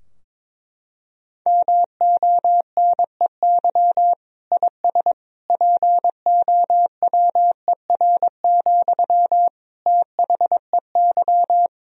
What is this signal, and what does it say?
Morse code, heard as a keyed tone, about 22 wpm, 700 Hz. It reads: MONEY IS POWER, THEY
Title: Sentences from Top 1000 Words - Review Part 2 - 22wpm